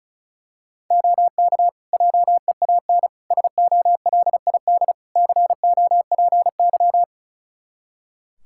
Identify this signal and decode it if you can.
Morse code, heard as a keyed tone, about 36 wpm, 700 Hz. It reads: OK JEAN SOLID COPY
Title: Practice QSO - Exchange 3 - More Info - 35wpm